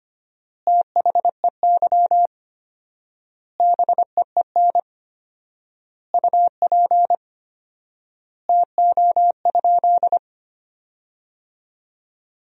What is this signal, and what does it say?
Morse code, heard as a keyed tone, about 25 wpm, 700 Hz. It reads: THEY BEEN UP TO?
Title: 4x Word-Spacing - Sentences from Top 100 Words - 25wpm